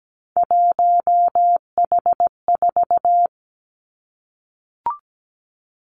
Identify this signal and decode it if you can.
Morse code, heard as a keyed tone, about 17 wpm, 700 Hz. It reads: 1H4
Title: Three Challenging Letter-Number Combinations 17wpm